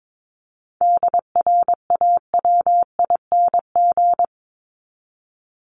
Morse code, heard as a keyed tone, about 22 wpm, 700 Hz. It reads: DRAWING
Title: Top 1751-2000 Words 22wpm